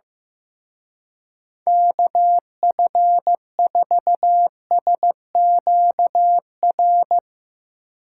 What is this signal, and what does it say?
Morse code, heard as a keyed tone, about 15 wpm, 700 Hz. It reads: KF4SQR
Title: Speed Racing - USA 2x3 Call Signs 15wpm